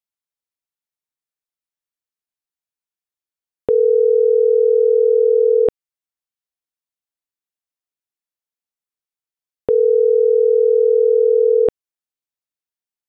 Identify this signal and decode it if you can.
Ringback tone (North America), 440 + 480 Hz.